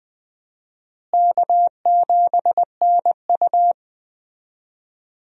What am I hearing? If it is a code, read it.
Morse code, heard as a keyed tone, about 21 wpm, 700 Hz. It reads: K7NU
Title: USA Call Signs - 4 Characters - Repeated 3x - 20wpm